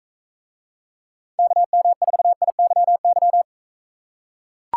Morse code, heard as a keyed tone, about 43 wpm, 700 Hz. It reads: KM4IYY